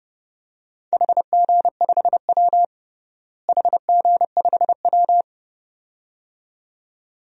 Morse code, heard as a keyed tone, about 30 wpm, 700 Hz. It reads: HG5W HG5W
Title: DXCC-100 Call Signs - 4 Characters - Repeated 2x - 30wpm